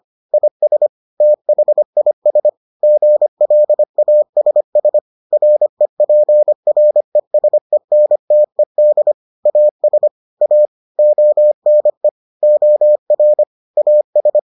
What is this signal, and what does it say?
Morse code, heard as a keyed tone, about 25 wpm, 600 Hz. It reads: IS THIS GLASS REPRESENTED AS A ONE OR AS